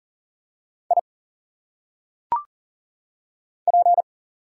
Morse code, heard as a keyed tone, about 40 wpm, 700 Hz. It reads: I P
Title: Single Letters - Phonetic Pronunciation 40wpm